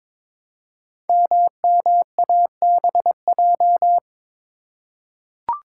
Morse code, heard as a keyed tone, about 22 wpm, 700 Hz. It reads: MMABJ
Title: Code Groups - 5 Letter:Numbers - 22wpm